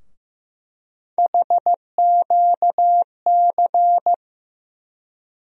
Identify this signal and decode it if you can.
Morse code, heard as a keyed tone, about 15 wpm, 700 Hz. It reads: HQC
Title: USA Call Sign Postfixes - 3 Letters 15wpm